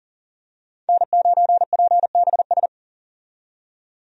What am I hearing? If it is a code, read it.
Morse code, heard as a keyed tone, about 37 wpm, 700 Hz. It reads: N9PBS
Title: USA Call Signs - 5 Characters - 40wpm